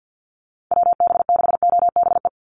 Morse code, heard as a keyed tone, about 50 wpm, 700 Hz. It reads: WB6OBE
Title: USA Call Signs 50wpm